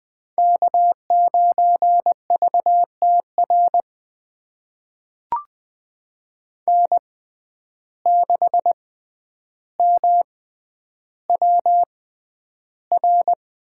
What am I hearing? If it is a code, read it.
Morse code, heard as a keyed tone, about 20 wpm, spaced as for 10 wpm, 700 Hz. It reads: K9VTR N6MWR
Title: USA Call Signs - 5 Characters - 6x Character Spacing - 20wpm